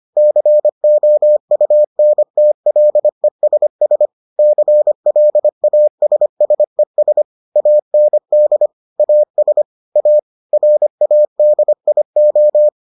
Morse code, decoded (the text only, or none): COUNTLESS CLASSES AND AS A RADIO